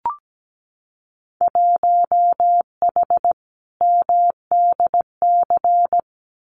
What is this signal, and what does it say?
Morse code, heard as a keyed tone, about 17 wpm, 700 Hz. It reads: 1H MDC
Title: Winter Field Day Exchanges 17wpm